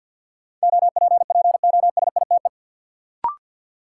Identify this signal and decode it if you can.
Morse code, heard as a keyed tone, about 49 wpm, 700 Hz. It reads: OPPOSITE
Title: Sentences from Top 1300 Words - New Word Drill Alt - 50wpm